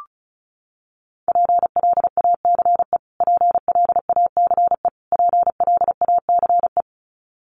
Morse code, heard as a keyed tone, about 34 wpm, 700 Hz. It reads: PLACE PLACE PLACE
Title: Top 300 Words Repeated 35wpm